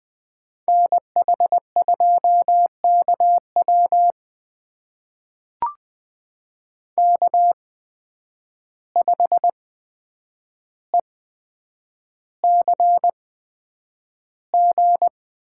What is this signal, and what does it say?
Morse code, heard as a keyed tone, about 20 wpm, spaced as for 9 wpm, 700 Hz. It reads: NH2KW K5ECG
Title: USA Call Signs - 5 Characters - 8x Character Spacing - 20wpm